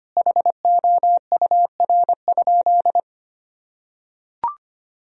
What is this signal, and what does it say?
Morse code, heard as a keyed tone, about 25 wpm, 700 Hz. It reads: HOUR?
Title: Sentences from Top 1000 Words - Review Part 4 - 25wpm